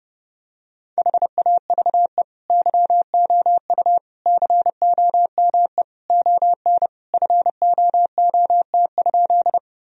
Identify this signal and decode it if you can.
Morse code, heard as a keyed tone, about 30 wpm, 700 Hz. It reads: HAVE YOU COME ON FOOT?
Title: Sentences from Top 1000 Words - Review Part 3 - 30wpm